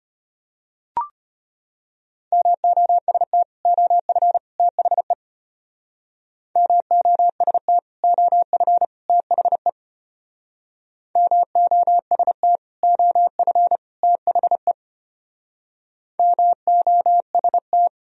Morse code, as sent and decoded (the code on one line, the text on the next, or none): -- --- ... - / --- ..-. / - .... / -- --- ... - / --- ..-. / - .... / -- --- ... - / --- ..-. / - .... / -- --- ... -
MOST OF THE MOST OF THE MOST OF THE MOST